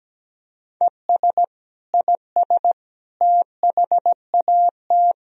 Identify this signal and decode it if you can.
Morse code, heard as a keyed tone, about 17 wpm, 700 Hz. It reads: ES IS THAT
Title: Sentences from Top 100 Words and Abbreviations 17wpm